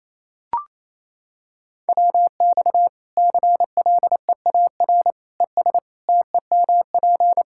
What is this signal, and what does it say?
Morse code, heard as a keyed tone, about 27 wpm, 700 Hz. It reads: WX CLEAR ES TEMP
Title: Practice QSO - Exchange 2 - Info - 28wpm